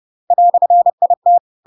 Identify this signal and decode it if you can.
Morse code, heard as a keyed tone, about 30 wpm, 700 Hz. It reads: "IT
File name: Pinocchio / Pinocchio ch6 30 30 ik1hge